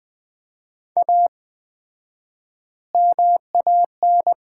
Morse code, heard as a keyed tone, about 20 wpm, 700 Hz. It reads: A MAN
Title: 4x Word-Spacing - Sets of 5 Words - 20wpm